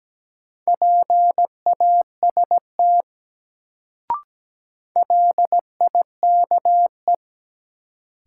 Morse code, heard as a keyed tone, about 17 wpm, 700 Hz. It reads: PAST LIKE
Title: Sets of 2 Words - New Word Drill - 17wpm